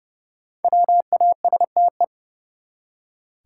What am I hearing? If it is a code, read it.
Morse code, heard as a keyed tone, about 30 wpm, 700 Hz. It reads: WASTE
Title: Five Letter Words - 30wpm